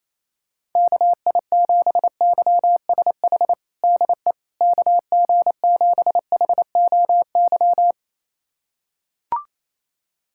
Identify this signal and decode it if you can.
Morse code, heard as a keyed tone, about 28 wpm, 700 Hz. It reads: KI7YSH DE KG7HOY
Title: USA Callsign DE Callsign 28wpm